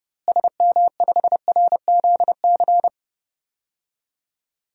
Morse code, heard as a keyed tone, about 31 wpm, 700 Hz. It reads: SM5RZC
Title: DXCC-100 Call Signs - 6 Characters - Repeated 2x - 30wpm